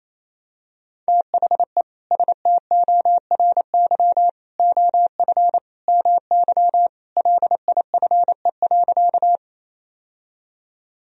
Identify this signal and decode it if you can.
Morse code, heard as a keyed tone, about 28 wpm, 700 Hz. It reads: THE STORY OF MY LIFE.
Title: Sentences from Top 1000 Words - Review Part 1 - 28wpm